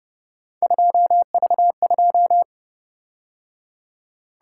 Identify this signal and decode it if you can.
Morse code, heard as a keyed tone, about 29 wpm, 700 Hz. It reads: 2V2